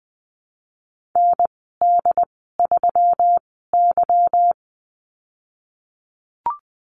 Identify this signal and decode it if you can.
Morse code, heard as a keyed tone, about 20 wpm, spaced as for 15 wpm, 700 Hz. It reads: ND3Y
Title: 173 Lesson 28g - Number 9 - Call Signs 20wpm